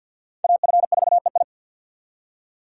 Morse code, heard as a keyed tone, about 47 wpm, 700 Hz. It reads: AF4EI
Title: USA Call Signs - 5 Characters - 4x Character Spacing - 50wpm